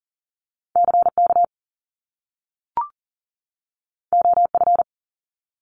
Morse code, heard as a keyed tone, about 41 wpm, 700 Hz. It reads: CK OF
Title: Top Two Letter Combinations in Words 40wpm